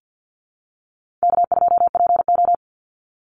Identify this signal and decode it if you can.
Morse code, heard as a keyed tone, about 50 wpm, 700 Hz. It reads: K2PO
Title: Easy Call Signs - 50wpm